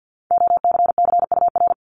Morse code, heard as a keyed tone, about 50 wpm, 700 Hz. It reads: OCCUR
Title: Multi-letter Phonemes 50wpm